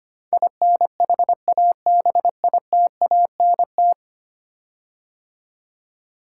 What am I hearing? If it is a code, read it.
Morse code, heard as a keyed tone, about 25 wpm, 700 Hz. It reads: INHABITANT
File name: 10 Letter Words - First 3 Letters Given - 25wpm